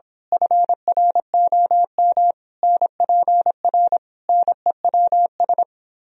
Morse code, heard as a keyed tone, about 26 wpm, 700 Hz. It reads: FROM NPR NEWS
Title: Speed-Racing - Sets of 5 Words 22wpm